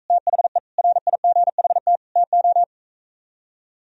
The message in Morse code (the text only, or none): THE RIGHT TO